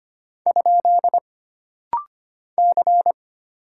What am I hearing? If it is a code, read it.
Morse code, heard as a keyed tone, about 25 wpm, 700 Hz. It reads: ? C
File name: Single Character (Letter:Number:Punctuation:Prosign) - Rapid-Fire - 25wpm